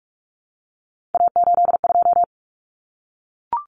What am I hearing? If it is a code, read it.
Morse code, heard as a keyed tone, about 44 wpm, 700 Hz. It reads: A82
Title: Common Cut Numbers - 3-Digit 45wpm